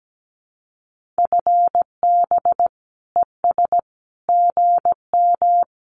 Morse code, heard as a keyed tone, about 17 wpm, 700 Hz. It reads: FB ES GM
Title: Common QSO Phrases Encore 17wpm